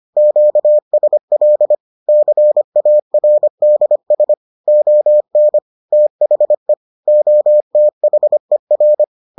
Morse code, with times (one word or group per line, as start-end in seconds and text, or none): 0.2-1.8 QSL
2.1-4.3 CARDS
4.7-5.6 ON
5.9-6.7 THE
7.1-9.0 OTHER